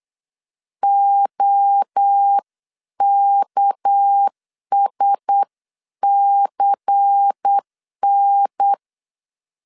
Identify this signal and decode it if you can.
Morse code, heard as a keyed tone, about 8 wpm, 800 Hz. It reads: OKSCN
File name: 8wpm